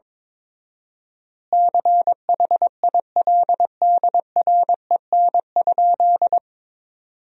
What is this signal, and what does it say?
Morse code, heard as a keyed tone, about 21 wpm, 700 Hz. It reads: CHILDREN?
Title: 4x Word-Spacing - Sentences from Top 300 Words - 22wpm